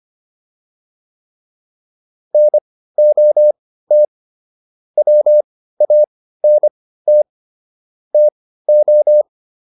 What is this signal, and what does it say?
Morse code, heard as a keyed tone, about 25 wpm, spaced as for 15 wpm, 600 Hz. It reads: NOT WANT TO